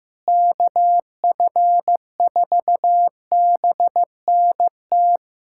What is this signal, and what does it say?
Morse code, heard as a keyed tone, about 15 wpm, 700 Hz. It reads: KF4BNT
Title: USA Callsign DE Callsign - 6 Characters - 15wpm